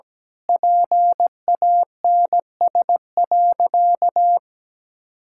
Morse code, heard as a keyed tone, about 17 wpm, 700 Hz. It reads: PANS.
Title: Pangram Challenge v2 - 17wpm